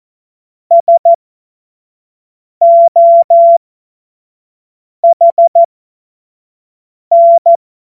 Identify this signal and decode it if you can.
Morse code, heard as a keyed tone, about 14 wpm, spaced as for 5 wpm, 680 Hz. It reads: SOHN
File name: Morsekurs / Witze / Witz 28. Dezember 2021, 25 F70